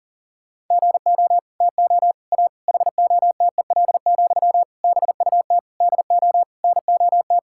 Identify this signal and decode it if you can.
Morse code, heard as a keyed tone, about 40 wpm, 700 Hz. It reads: GO TO A HOTEL, BUT DO NOT
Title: Sentences from Top 1600 Words 40wpm